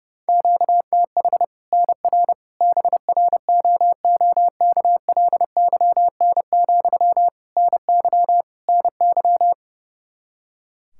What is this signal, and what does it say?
Morse code, heard as a keyed tone, about 30 wpm, 700 Hz. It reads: QTH NR BROOKLYN, NY NY
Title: Practice QSO - Exchange 1 - Beginning - 30wpm